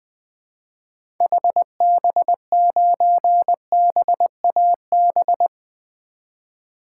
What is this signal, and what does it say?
Morse code, heard as a keyed tone, about 20 wpm, 700 Hz. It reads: HB9BAB
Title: 2023 SOTA Activator Call Signs - 20wpm